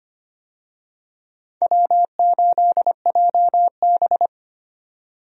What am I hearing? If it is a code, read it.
Morse code, heard as a keyed tone, about 25 wpm, 700 Hz. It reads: W8JB
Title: USA Call Signs - 4 Characters - 25wpm